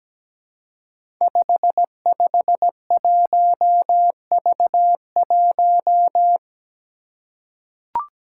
Morse code, heard as a keyed tone, about 17 wpm, 700 Hz. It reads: 551V1
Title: Five Challenging Letter-Number Combinations 17wpm